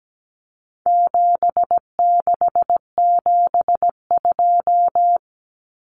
Morse code, heard as a keyed tone, about 17 wpm, 700 Hz. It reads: 7672
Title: Common Cut Numbers - 4-Digit 17wpm